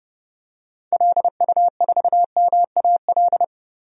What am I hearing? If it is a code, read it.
Morse code, heard as a keyed tone, about 31 wpm, 700 Hz. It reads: LU4MAL
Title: DXCC-100 Call Signs - 6 Characters - Repeated 3x - 30wpm